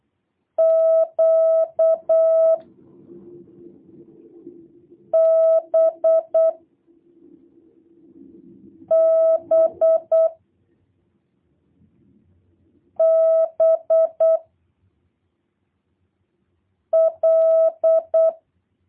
Morse code, heard as a keyed tone, about 8 wpm, 660 Hz. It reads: Q B B B L
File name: jh8cbh / 2023cw / review1-6